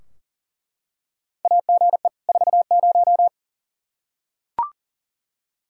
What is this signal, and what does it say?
Morse code, heard as a keyed tone, about 40 wpm, 700 Hz. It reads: AGE 40
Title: K1USN SST QSO Practice 40wpm